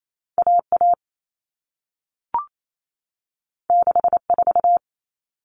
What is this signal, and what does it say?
Morse code, heard as a keyed tone, about 28 wpm, 700 Hz. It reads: AA 64
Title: All Cut Numbers - 2-Digit 28wpm